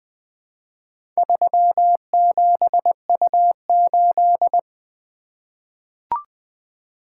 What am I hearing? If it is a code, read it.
Morse code, heard as a keyed tone, about 20 wpm, 700 Hz. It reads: 37U8
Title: Four Letter-Number Combinations - 20wpm